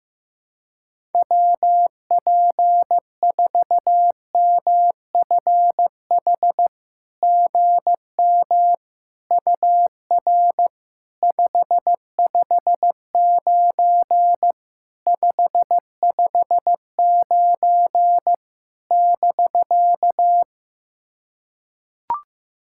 Morse code, heard as a keyed tone, about 15 wpm, 700 Hz. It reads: WP4MFH GM UR 559 559 <BK>
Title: SOTA QSO Practice 15wpm